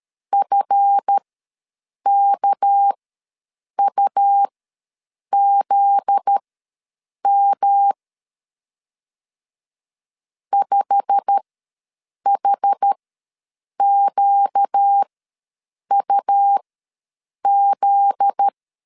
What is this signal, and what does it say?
Morse code, heard as a keyed tone, about 13 wpm, spaced as for 7 wpm, 800 Hz. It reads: FKUZM 5HQUZ
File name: LEZIONI / 8 YJ45 / 8 YJ45 ws 12wpm 3146